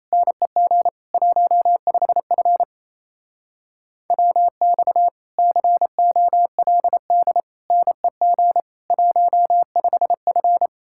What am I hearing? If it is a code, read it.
Morse code, heard as a keyed tone, about 36 wpm, 700 Hz. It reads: NEG 15F WX COLD NEG 15F